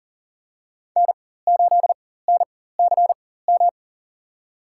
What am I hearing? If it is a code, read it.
Morse code, heard as a keyed tone, about 37 wpm, spaced as for 18 wpm, 700 Hz. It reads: N8NCM
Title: USA Call Signs - 5 Characters - 4x Character Spacing - 40wpm